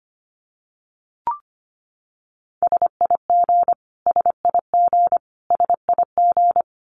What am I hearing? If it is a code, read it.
Morse code, heard as a keyed tone, about 25 wpm, 700 Hz. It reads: SIG SIG SIG
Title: Abbreviations Repeated 3x - 25wpm